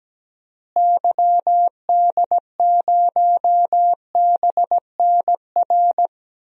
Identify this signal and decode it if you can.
Morse code, heard as a keyed tone, about 17 wpm, 700 Hz. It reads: YD0BNR
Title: DXCC-100 Call Signs - 6 Characters - Repeated 3x - 17wpm